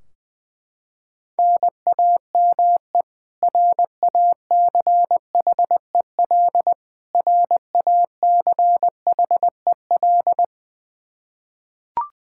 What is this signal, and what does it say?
Morse code, heard as a keyed tone, about 21 wpm, 700 Hz. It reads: NAME RACHEL RACHEL
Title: Practice QSO - Exchange 1 - Beginning - 20wpm